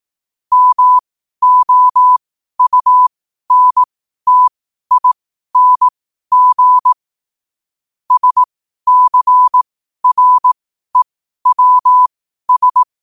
Morse code, heard as a keyed tone, about 18 wpm, spaced as for 12 wpm, 1000 Hz. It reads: MOUNTING SCREWS
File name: qotdfiles / 12wpm